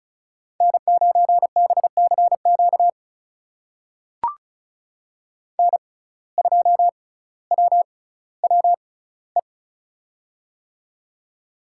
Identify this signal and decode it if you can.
Morse code, heard as a keyed tone, about 36 wpm, spaced as for 18 wpm, 700 Hz. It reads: N9BCQ N2WWE
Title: USA Call Signs - 5 Characters - 6x Character Spacing - 35wpm